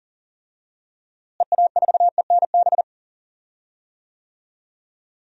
Morse code, heard as a keyed tone, about 40 wpm, 700 Hz. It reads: EA4ENB
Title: DXCC-100 Call Signs - 6 Characters - Repeated 3x - 40wpm